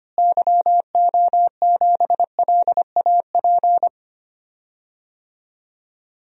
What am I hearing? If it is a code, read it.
Morse code, heard as a keyed tone, about 25 wpm, 700 Hz. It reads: YO7LAP
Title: DXCC-100 Call Signs - 6 Characters - Repeated 3x - 25wpm